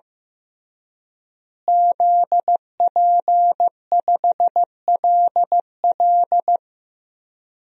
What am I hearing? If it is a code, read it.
Morse code, heard as a keyed tone, about 15 wpm, 700 Hz. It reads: ZP5LL
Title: DXCC-100 Call Signs - 5 Characters - Repeated 3x - 15wpm